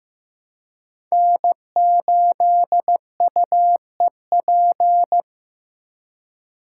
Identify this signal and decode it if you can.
Morse code, heard as a keyed tone, about 15 wpm, 700 Hz. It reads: N8UEP
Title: Code Groups - 5 Letter:Numbers - 15wpm